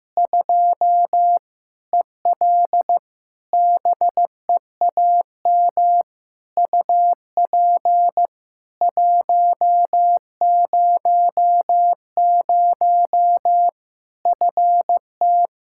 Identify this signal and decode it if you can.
Morse code, heard as a keyed tone, about 15 wpm, 700 Hz. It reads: 2 EL BEAM UP 100 FT
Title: Practice QSO - Exchange 2 - Info - 15wpm